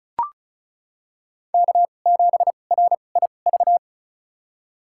Morse code, heard as a keyed tone, about 36 wpm, spaced as for 25 wpm, 700 Hz. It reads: K7RIV
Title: USA Call Signs - 5 Characters - 2x Character Spacing - 35wpm